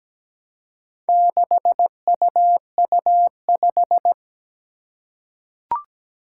Morse code, heard as a keyed tone, about 17 wpm, 700 Hz. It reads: 6UU5
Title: Four Challenging Letter-Number Combinations 17wpm